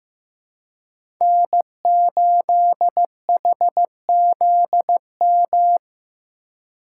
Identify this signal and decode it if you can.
Morse code, heard as a keyed tone, about 15 wpm, 700 Hz. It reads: N8HZM
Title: K1USN SST QSO Practice - 15wpm